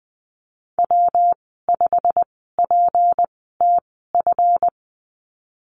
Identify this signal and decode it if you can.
Morse code, heard as a keyed tone, about 20 wpm, spaced as for 15 wpm, 700 Hz. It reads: W5PTF
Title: 193 Lesson 31g - Letter v - Call Signs 20wpm